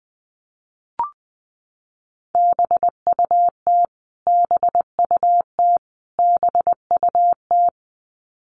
Morse code, heard as a keyed tone, about 20 wpm, 700 Hz. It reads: BUT BUT BUT
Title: Top 500 Words Repeated 20wpm